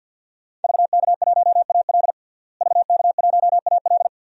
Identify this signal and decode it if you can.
Morse code, heard as a keyed tone, about 52 wpm, 700 Hz. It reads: VK1AL VK1AL
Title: DXCC-100 Call Signs - 5 Characters - Repeated 3x - 50wpm